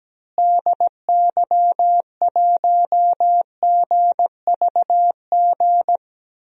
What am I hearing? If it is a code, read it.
Morse code, heard as a keyed tone, about 17 wpm, 700 Hz. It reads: DY1GVG
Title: DXCC-100 Call Signs - 6 Characters - Repeated 2x - 17wpm